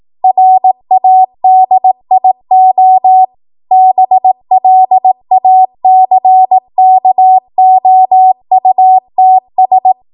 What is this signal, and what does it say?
Morse code, heard as a keyed tone, about 18 wpm, 750 Hz. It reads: RADIO BLACKOUTS
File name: Morse / 18WPM